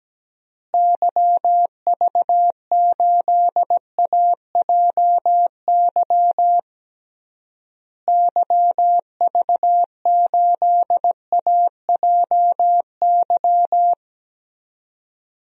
Morse code, heard as a keyed tone, about 17 wpm, 700 Hz. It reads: YV8AJY YV8AJY
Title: DXCC-100 Call Signs - 6 Characters - Repeated 3x - 17wpm